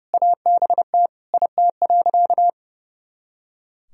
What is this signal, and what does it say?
Morse code, heard as a keyed tone, about 32 wpm, 700 Hz. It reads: ABT IT.
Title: Sentences from Top 200 Words and Abbreviations 30wpm